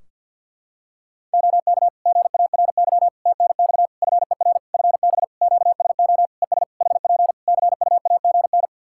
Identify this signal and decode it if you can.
Morse code, heard as a keyed tone, about 50 wpm, 700 Hz. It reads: OK GARY TNX FER FB QSO ES HP CUAGN
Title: Practice QSO - Exchange 4 - Ending - 50wpm